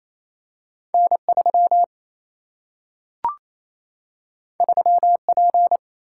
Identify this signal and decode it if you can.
Morse code, heard as a keyed tone, about 28 wpm, 700 Hz. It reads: N 3 3 P
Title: Single-Letter Single-Number 28wpm